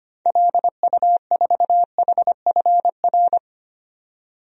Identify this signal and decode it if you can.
Morse code, heard as a keyed tone, about 25 wpm, 690 Hz. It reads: LU4HFR
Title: DXCC-100 Call Signs - 6 Characters - 25wpm